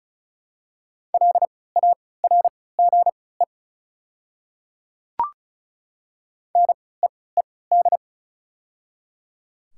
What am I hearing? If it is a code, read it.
Morse code, heard as a keyed tone, about 36 wpm, spaced as for 20 wpm, 700 Hz. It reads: LARGE NEED
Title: 3x Character Spacing - Top 200 Words - 35wpm